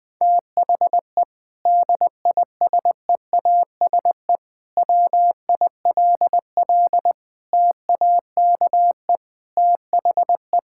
Morse code, heard as a keyed tone, about 20 wpm, 700 Hz. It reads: THE DISEASE WILL TAKE THE
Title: Sentences from Top 1000 Words - Review Part 6 - 20wpm